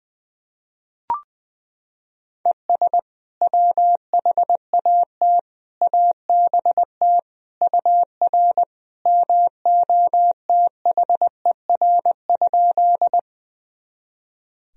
Morse code, heard as a keyed tone, about 21 wpm, 700 Hz. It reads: ES WHAT ABT UR MOTHER?
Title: Sentences from Top 200 Words and Abbreviations 20wpm